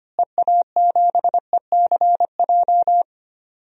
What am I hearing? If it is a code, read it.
Morse code, heard as a keyed tone, about 25 wpm, 700 Hz. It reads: EA7ECJ
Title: DXCC-100 Call Signs - 6 Characters - Repeated 3x - 25wpm